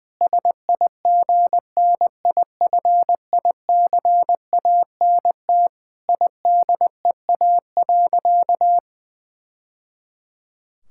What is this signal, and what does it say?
Morse code, heard as a keyed tone, about 20 wpm, 700 Hz. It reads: SIGNIFICANT IDEA.
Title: Sentences from Top 1000 Words - Review Part 6 - 20wpm